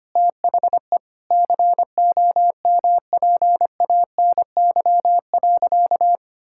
Morse code, heard as a keyed tone, about 25 wpm, 700 Hz. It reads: THE COMPANY.
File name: practice / 25wpm / Sentences with 3-Word Phrases - 25wpm